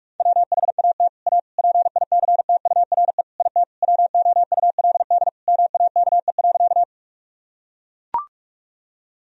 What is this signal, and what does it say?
Morse code, heard as a keyed tone, about 41 wpm, 700 Hz. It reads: WHAT A PICTURE IT WOULD MAKE.
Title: Sentences from Top 1000 Words - Review Part 1 - 45wpm